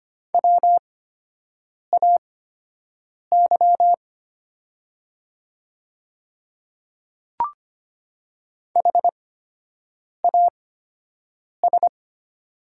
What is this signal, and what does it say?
Morse code, heard as a keyed tone, about 25 wpm, spaced as for 7 wpm, 700 Hz. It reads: WAY HAS
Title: 8x Character Spacing - Top 100 Words - 25wpm